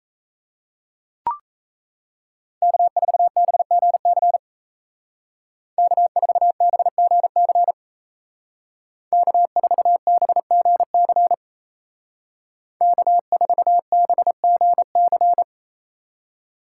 Morse code, heard as a keyed tone, about 33 wpm, 700 Hz. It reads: K4BGC K4BGC K4BGC K4BGC
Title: Speed Racing - USA 1x3 Call Signs 28wpm